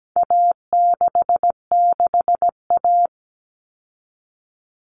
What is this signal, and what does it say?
Morse code, heard as a keyed tone, about 17 wpm, 700 Hz. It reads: A66A
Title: All Cut Numbers - 4-Digit 17wpm